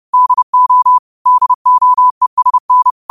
Morse code, heard as a keyed tone, about 30 wpm, 1000 Hz. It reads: DO DOESN